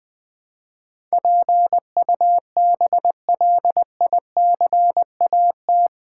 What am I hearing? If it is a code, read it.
Morse code, heard as a keyed tone, about 20 wpm, 700 Hz. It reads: PUBLICAT
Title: Build A Word - 4 Letters At A Time - 11-Letter Words - 20wpm